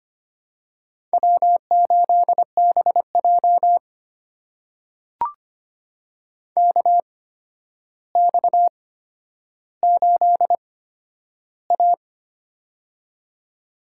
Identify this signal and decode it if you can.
Morse code, heard as a keyed tone, about 25 wpm, spaced as for 11 wpm, 700 Hz. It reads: W8BJ KX8A
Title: USA Call Signs - 4 Characters - 8x Character Spacing - 25wpm